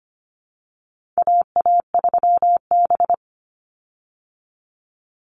Morse code, heard as a keyed tone, about 25 wpm, 700 Hz. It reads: AA3B
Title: Call Signs Repeated 3x - 25wpm